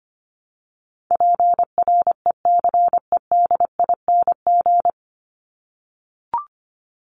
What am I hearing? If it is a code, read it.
Morse code, heard as a keyed tone, about 25 wpm, 700 Hz. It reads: PRECEDING
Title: ING Suffix - Top 100 Words with Base Words 25wpm